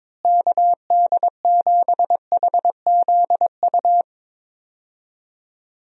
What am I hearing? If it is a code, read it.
Morse code, heard as a keyed tone, about 22 wpm, 700 Hz. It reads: KD7HZU
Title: USA Call Signs - 6 Characters - Repeated 3x - 22wpm